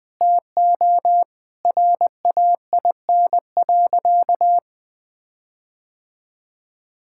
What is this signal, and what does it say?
Morse code, heard as a keyed tone, about 20 wpm, 700 Hz. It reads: TO RAIN.
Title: Sentences with 4-Word Phrases - Spoken Beginning - 20wpm